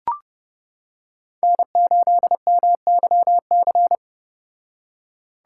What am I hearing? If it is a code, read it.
Morse code, heard as a keyed tone, about 29 wpm, 700 Hz. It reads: N8MYC